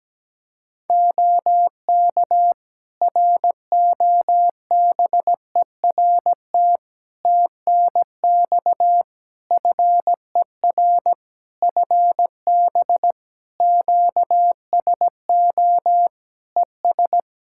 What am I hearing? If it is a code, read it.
Morse code, heard as a keyed tone, about 17 wpm, 700 Hz. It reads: OK ROBERT TNX FER FB QSO ES
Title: Practice QSO - Exchange 4 - Ending - 17wpm